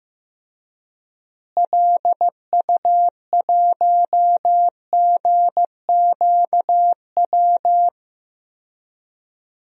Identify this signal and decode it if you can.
Morse code, heard as a keyed tone, about 16 wpm, 700 Hz. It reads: LU1GQW
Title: DXCC-100 Call Signs - 6 Characters - Repeated 2x - 15wpm